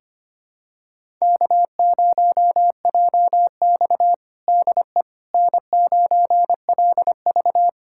Morse code, heard as a keyed tone, about 25 wpm, 700 Hz. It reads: K0JX DE N9LV
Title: USA Callsign DE Callsign - 4 Characters - 25wpm